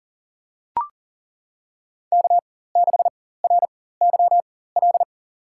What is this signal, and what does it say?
Morse code, heard as a keyed tone, about 40 wpm, spaced as for 19 wpm, 700 Hz. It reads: K6RYL
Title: USA Call Signs - 5 Characters - 4x Character Spacing - 40wpm